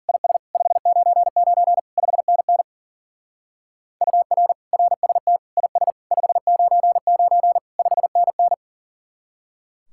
Morse code, heard as a keyed tone, about 50 wpm, 700 Hz. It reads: IS 599 5NN UR RST IS 599 5NN